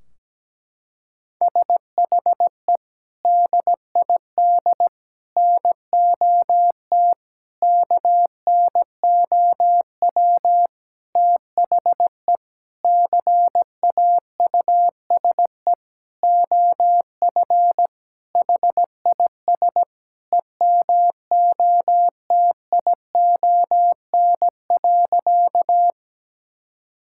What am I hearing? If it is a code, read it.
Morse code, heard as a keyed tone, about 17 wpm, 700 Hz. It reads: SHE DID NOT KNOW THE CAUSE OF HIS EMOTION.
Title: Sentences from Top 1600 Words 17wpm